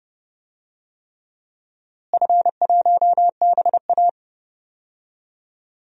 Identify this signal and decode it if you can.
Morse code, heard as a keyed tone, about 31 wpm, 700 Hz. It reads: F1BA
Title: DXCC-100 Call Signs - 4 Characters - Repeated 3x - 30wpm